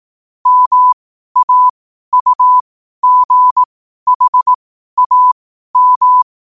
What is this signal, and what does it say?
Morse code, heard as a keyed tone, about 18 wpm, spaced as for 12 wpm, 1000 Hz. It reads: MAUGHAM
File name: qotdfiles / 12wpm